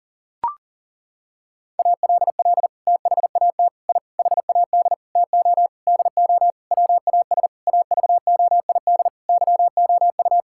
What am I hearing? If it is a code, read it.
Morse code, heard as a keyed tone, about 37 wpm, 700 Hz. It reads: ALL THAT I HAD TO DO WAS AVOID YOU
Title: Sentences from Top 1000 Words - Review Part 6 - 40wpm